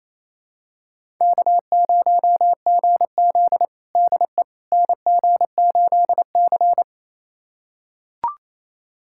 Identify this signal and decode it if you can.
Morse code, heard as a keyed tone, about 28 wpm, 700 Hz. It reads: K0GZ DE NG8C
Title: USA Callsign DE Callsign - 4 Characters - 28wpm